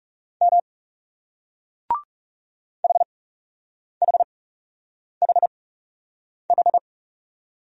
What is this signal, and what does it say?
Morse code, heard as a keyed tone, about 40 wpm, spaced as for 8 wpm, 700 Hz. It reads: M HHHH